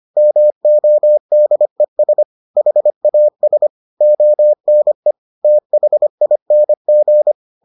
Morse code, heard as a keyed tone, about 25 wpm, 600 Hz. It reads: MODES HAS ONE THING